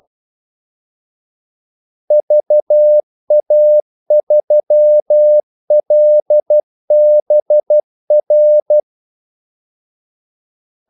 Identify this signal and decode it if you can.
Morse code, heard as a keyed tone, about 12 wpm, 600 Hz. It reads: VA3LBR
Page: 2 ekstra ordmellemrum mellem kaldesignalerne